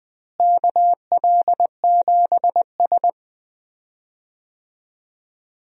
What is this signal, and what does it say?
Morse code, heard as a keyed tone, about 20 wpm, 700 Hz. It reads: KL7S
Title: DXCC-100 Call Signs - 4 Characters - Repeated 3x - 20wpm